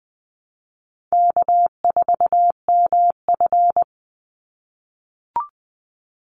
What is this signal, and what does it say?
Morse code, heard as a keyed tone, about 19 wpm, 700 Hz. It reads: K4MF
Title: NA SOTA Chaser Callsigns 20wpm